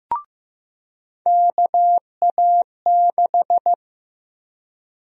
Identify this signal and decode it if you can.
Morse code, heard as a keyed tone, about 15 wpm, 700 Hz. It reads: KA6
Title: USA Call Sign Prefixes - 15wpm